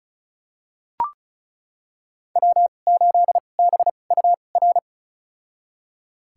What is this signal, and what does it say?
Morse code, heard as a keyed tone, about 36 wpm, spaced as for 25 wpm, 700 Hz. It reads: W8BUR
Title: USA Call Signs - 2x Character Spacing - 35wpm